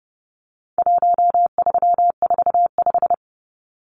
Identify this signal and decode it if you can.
Morse code, heard as a keyed tone, about 30 wpm, 700 Hz. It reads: 1345
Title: Numbers - 4-Digit 30wpm